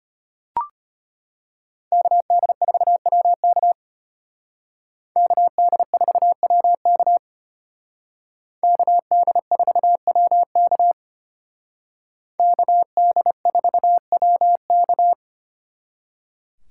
Morse code, heard as a keyed tone, about 33 wpm, 700 Hz. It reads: KD4WK KD4WK KD4WK KD4WK